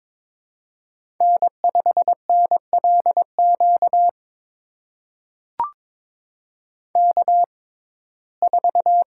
Morse code, heard as a keyed tone, about 22 wpm, spaced as for 16 wpm, 700 Hz. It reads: N5NLQ K4
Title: USA Call Signs - 5 Characters - 6x Character Spacing - 22wpm